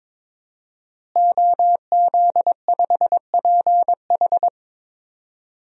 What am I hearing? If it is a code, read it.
Morse code, heard as a keyed tone, about 22 wpm, 700 Hz. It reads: OZ5PH
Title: DXCC-100 Call Signs - 5 Characters - Repeated 3x - 22wpm